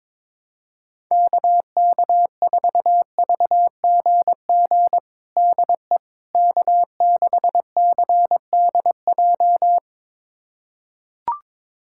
Morse code, heard as a keyed tone, about 22 wpm, 700 Hz. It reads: KK4VGG DE K6CDJ
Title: USA Callsign DE Callsign 22wpm